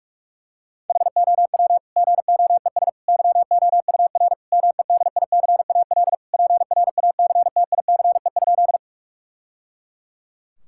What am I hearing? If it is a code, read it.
Morse code, heard as a keyed tone, about 41 wpm, 700 Hz. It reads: HOW GOES YOUR MEDICAL PRACTICE?
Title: Sentences from Top 1000 Words - Review Part 5 - 45wpm